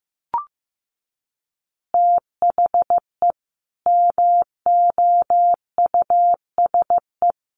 Morse code, heard as a keyed tone, about 15 wpm, 700 Hz. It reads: THE MOUSE
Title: Easy Phrases 15wpm